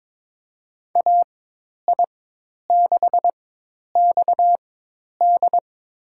Morse code, heard as a keyed tone, about 22 wpm, spaced as for 10 wpm, 700 Hz. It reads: AI6XD
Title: USA Call Signs - 5 Characters - 4x Character Spacing - 22wpm